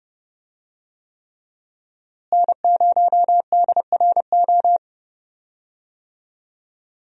Morse code, heard as a keyed tone, about 32 wpm, 700 Hz. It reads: N0DRO
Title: USA Call Signs - 5 Characters - Repeated 3x - 30wpm